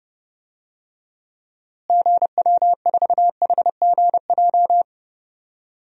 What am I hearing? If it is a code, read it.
Morse code, heard as a keyed tone, about 30 wpm, 700 Hz. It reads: GW4HGJ
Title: DXCC-100 Call Signs - 6 Characters - Repeated 2x - 30wpm